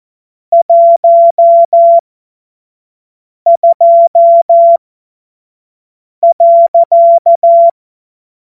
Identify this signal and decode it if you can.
Morse code, heard as a keyed tone, about 14 wpm, spaced as for 5 wpm, 680 Hz. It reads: 12.
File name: Morsekurs / Witze / Witz 19 Dezember 2023 25 F70